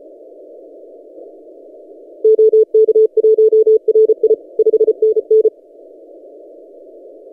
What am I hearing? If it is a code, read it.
Morse code, heard as a keyed tone, about 32 wpm, 440 Hz. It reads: OK1RI 5NN
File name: tx6g-austral-is-12cw